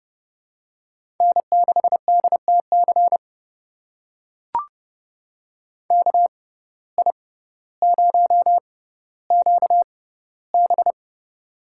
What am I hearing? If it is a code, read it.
Morse code, heard as a keyed tone, about 32 wpm, spaced as for 16 wpm, 700 Hz. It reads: N6DTC KI0QB
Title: USA Call Signs - 5 Characters - 6x Character Spacing - 30wpm